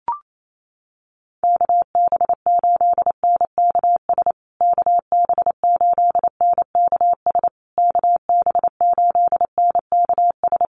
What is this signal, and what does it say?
Morse code, heard as a keyed tone, about 28 wpm, 700 Hz. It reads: KB8NKS KB8NKS KB8NKS
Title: USA Call Signs Repeated 3x 28wpm